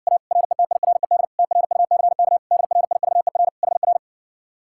Morse code, heard as a keyed tone, about 60 wpm, 700 Hz. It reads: A RETIRED TRUCK DRIVER HR